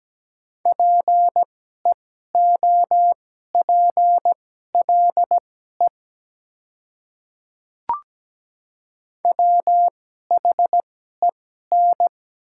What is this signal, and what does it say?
Morse code, heard as a keyed tone, about 17 wpm, spaced as for 12 wpm, 700 Hz. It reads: PEOPLE WHEN
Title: 2x Character Spacing - Top 100 Words - 17wpm